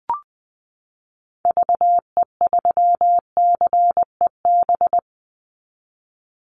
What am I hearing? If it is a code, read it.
Morse code, heard as a keyed tone, about 20 wpm, 700 Hz. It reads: VE3CEB
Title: Call Signs - 20wpm